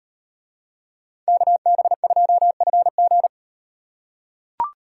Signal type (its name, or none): Morse code